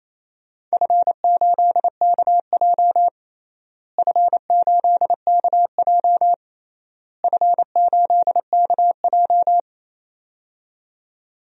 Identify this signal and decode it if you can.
Morse code, heard as a keyed tone, about 28 wpm, 700 Hz. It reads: F8KJ F8KJ F8KJ
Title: DXCC-100 Call Signs - 4 Characters - Repeated 3x - 28wpm